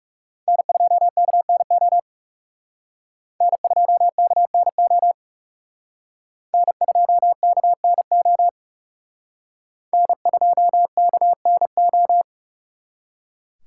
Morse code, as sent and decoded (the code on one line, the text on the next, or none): -. ..--- -.- -. --- / -. ..--- -.- -. --- / -. ..--- -.- -. --- / -. ..--- -.- -. ---
N2KNO N2KNO N2KNO N2KNO